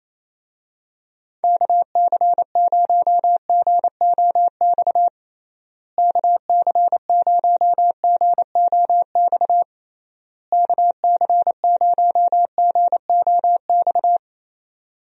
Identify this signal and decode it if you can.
Morse code, heard as a keyed tone, about 28 wpm, 700 Hz. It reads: KC0GOX KC0GOX KC0GOX
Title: USA Call Signs - 6 Characters - Repeated 3x - 28wpm